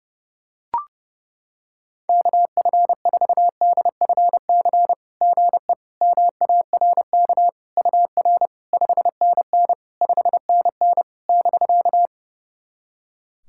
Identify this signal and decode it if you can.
Morse code, heard as a keyed tone, about 32 wpm, 700 Hz. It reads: KF4DFC GE MARK UR 5NN 5NN <BK>
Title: SOTA QSO Practice 30wpm